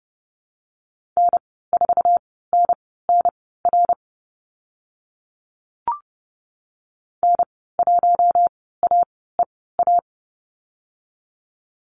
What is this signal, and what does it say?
Morse code, heard as a keyed tone, about 31 wpm, spaced as for 17 wpm, 700 Hz. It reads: N4NNR N1AEA
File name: Lesson 15g - Number 5 - Call Signs 30wpm